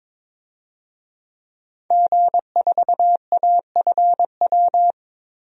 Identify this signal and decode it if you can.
Morse code, heard as a keyed tone, about 21 wpm, 700 Hz. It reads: G4AFW
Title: DXCC-100 Call Signs - 5 Characters - Repeated 3x - 22wpm